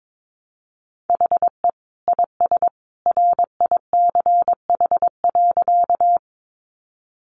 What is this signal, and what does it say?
Morse code, heard as a keyed tone, about 22 wpm, 700 Hz. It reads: HE IS RICH.
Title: Sentences from Top 1100 Words 22wpm